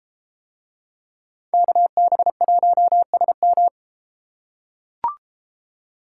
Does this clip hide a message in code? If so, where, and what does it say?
Yes: Morse code from 1.5 to 3.7 s, heard as a keyed tone, about 32 wpm, 700 Hz. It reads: KB1SM